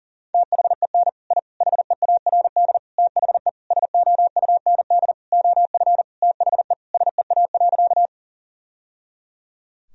Morse code, heard as a keyed tone, about 40 wpm, 700 Hz. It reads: THEN I HEARD THE SOUND OF THE SEA.
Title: Sentences from Top 1000 Words - Review Part 1 - 40wpm